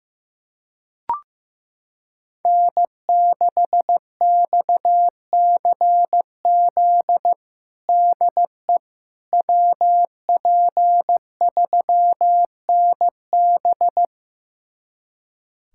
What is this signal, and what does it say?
Morse code, heard as a keyed tone, about 15 wpm, 700 Hz. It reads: N6XCZ DE WP3NB
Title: USA Callsign DE Callsign - 5 Characters - 15wpm